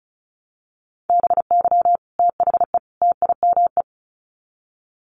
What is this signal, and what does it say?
Morse code, heard as a keyed tone, about 34 wpm, 700 Hz. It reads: BY THE TIME
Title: Sets of 3 Words 35wpm